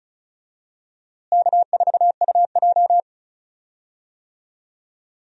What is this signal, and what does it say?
Morse code, heard as a keyed tone, about 34 wpm, 700 Hz. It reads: K4UJ
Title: USA Call Signs - 4 Characters - Repeated 3x - 35wpm